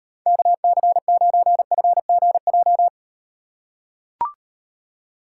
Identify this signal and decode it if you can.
Morse code, heard as a keyed tone, about 38 wpm, 700 Hz. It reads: KC9FGJ